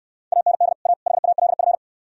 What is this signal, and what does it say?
Morse code, heard as a keyed tone, about 66 wpm, 700 Hz. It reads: AND A HALF